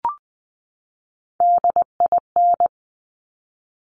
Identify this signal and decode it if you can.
Morse code, heard as a keyed tone, about 20 wpm, 700 Hz. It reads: DIN
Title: Top Three Letter Combinations in Words 20wpm